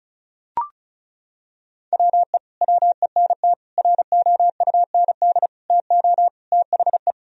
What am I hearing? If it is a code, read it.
Morse code, heard as a keyed tone, about 34 wpm, 700 Hz. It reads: WE WENT ROUND TO THE
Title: Sentences from Top 1000 Words - Review Part 1 - 35wpm